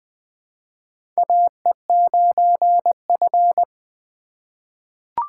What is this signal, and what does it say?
Morse code, heard as a keyed tone, about 20 wpm, 700 Hz. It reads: AE9F
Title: 2023 SOTA Chaser Call Signs - 20wpm